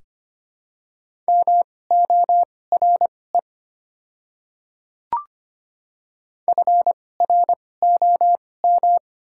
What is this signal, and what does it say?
Morse code, heard as a keyed tone, about 25 wpm, spaced as for 18 wpm, 700 Hz. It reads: MORE FROM
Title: 2x Character Spacing - Top 100 Words - 25wpm